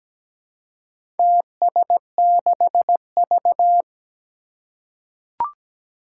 Morse code, heard as a keyed tone, about 17 wpm, 700 Hz. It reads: TS6V
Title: Four Challenging Letter-Number Combinations 17wpm